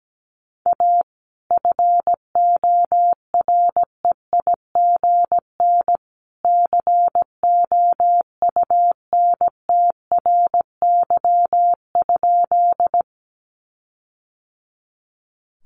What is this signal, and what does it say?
Morse code, heard as a keyed tone, about 17 wpm, 700 Hz. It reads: A FOREIGN COUNTRY?
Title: Sentences from Top 1400 Words 17wpm